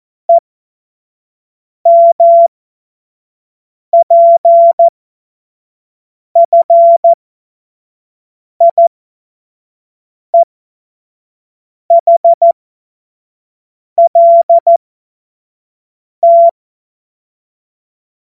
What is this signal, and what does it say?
Morse code, heard as a keyed tone, about 14 wpm, spaced as for 5 wpm, 680 Hz. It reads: EMPFIEHLT
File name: Morsekurs / Witze / Witz 14.November 2023 25 F70